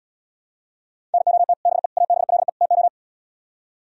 Morse code, heard as a keyed tone, about 79 wpm, 700 Hz. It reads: NOT BE ABLE TO